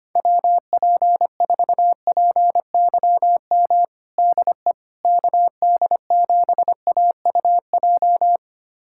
Morse code, heard as a keyed tone, about 25 wpm, 700 Hz. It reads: WP4PYM DE KD7AUJ
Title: USA Callsign DE Callsign - 6 Characters - 25wpm